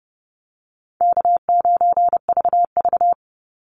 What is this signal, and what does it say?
Morse code, heard as a keyed tone, about 30 wpm, 700 Hz. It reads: K9VV
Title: Call Signs Repeated 2x - 30wpm